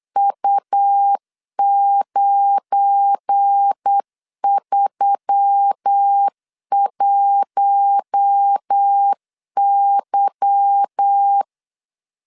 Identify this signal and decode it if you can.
Morse code, heard as a keyed tone, about 8 wpm, 800 Hz. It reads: U931Y
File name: LEZIONI / 8 YJ45 / 8 Y 8wpm 3245